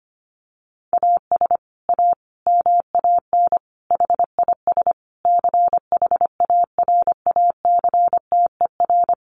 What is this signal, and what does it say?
Morse code, heard as a keyed tone, about 25 wpm, 700 Hz. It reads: AS A MAN HIS CHARACTER
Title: Sentences from Top 1200 Words 25wpm